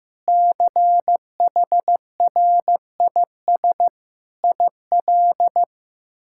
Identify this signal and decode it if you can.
Morse code, heard as a keyed tone, about 15 wpm, 700 Hz. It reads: CHRIS IL
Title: K1USN SST QSO Practice - 15wpm